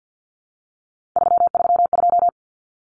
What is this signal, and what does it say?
Morse code, heard as a keyed tone, about 50 wpm, 700 Hz. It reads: 332
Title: Numbers - 3-Digit 50wpm